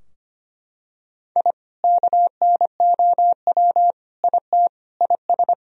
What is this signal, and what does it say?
Morse code, heard as a keyed tone, about 25 wpm, 690 Hz. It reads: I KNOW IT IS
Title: Sentences from Top 1000 Words - Review Part 4 - 25wpm